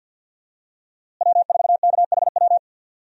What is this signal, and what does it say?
Morse code, heard as a keyed tone, about 52 wpm, 700 Hz. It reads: W4KHW
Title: USA Call Signs - 5 Characters - Repeated 2x - 50wpm